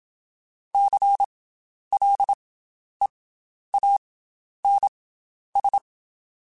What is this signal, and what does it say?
Morse code, heard as a keyed tone, about 27 wpm, 800 Hz. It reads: C L E A N S
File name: Special INGs / Special INGs-mp3s / Slow ing @25